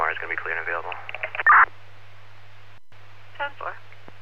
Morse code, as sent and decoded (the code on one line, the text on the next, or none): none